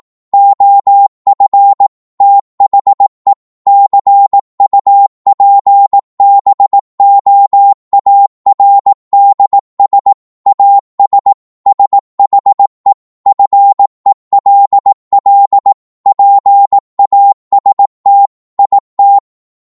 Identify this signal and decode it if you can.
Morse code, heard as a keyed tone, about 18 wpm, 800 Hz. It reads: OF THE CUPBOARDS AS SHE FELL PAST IT